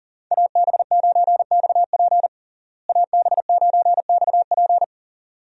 Morse code, heard as a keyed tone, about 40 wpm, 700 Hz. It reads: AB9XP AB9XP
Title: USA Call Signs - 5 Characters - Repeated 2x - 40wpm